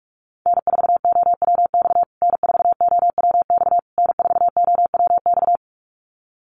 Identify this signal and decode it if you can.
Morse code, heard as a keyed tone, about 44 wpm, 700 Hz. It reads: N4OWX N4OWX N4OWX
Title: USA Call Signs Repeated 3x 45wpm